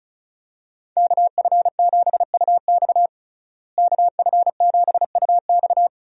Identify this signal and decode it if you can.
Morse code, heard as a keyed tone, about 36 wpm, 700 Hz. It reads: KF7UX KF7UX
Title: USA Call Signs - 5 Characters - Repeated 3x - 35wpm